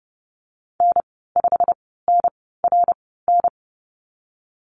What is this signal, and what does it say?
Morse code, heard as a keyed tone, about 30 wpm, spaced as for 17 wpm, 700 Hz. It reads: N5NRN
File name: Lesson 19g - Letter w - Call Signs 30wpm